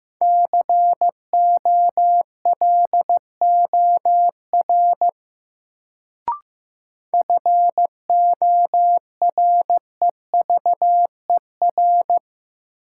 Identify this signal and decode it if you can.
Morse code, heard as a keyed tone, about 15 wpm, 700 Hz. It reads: COLOR FOREVER
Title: Sentences from Top 1600 Words - New Word Drill Alt - 15wpm